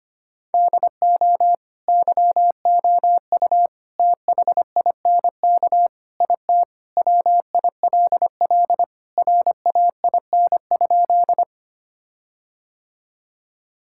Morse code, heard as a keyed tone, about 25 wpm, 700 Hz. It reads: DO YOU THINK IT WILL RAIN?
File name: practice / 25wpm / Sentences with 4-Word Phrases - 25wpm